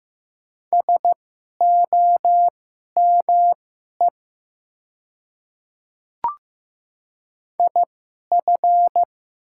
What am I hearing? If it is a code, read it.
Morse code, heard as a keyed tone, about 15 wpm, spaced as for 11 wpm, 700 Hz. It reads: SOME IF
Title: 2x Character Spacing - Top 200 Words - 15wpm